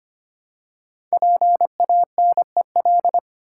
Morse code, heard as a keyed tone, about 25 wpm, 700 Hz. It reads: PANEL
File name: 5 Letter Words - First 3 Letters Given - 25wpm